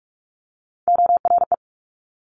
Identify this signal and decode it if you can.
Morse code, heard as a keyed tone, about 46 wpm, 700 Hz. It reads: ORE
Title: Multi-letter Phonemes Quick Fire 45wpm